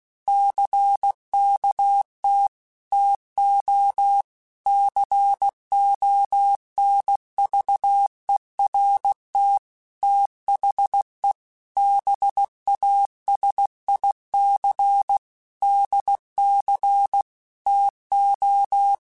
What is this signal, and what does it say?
Morse code, heard as a keyed tone, about 16 wpm, 800 Hz. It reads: CKT TO CONVERT THE BASIC DC TO
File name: Technical Subjects / Technical Subjects-mp3s / Sw Pwr Sup @15